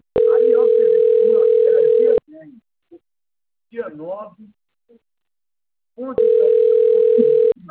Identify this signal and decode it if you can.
Ringback tone (North America), 440 + 480 Hz, with speech.
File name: rg-1600-unknown- - - .53816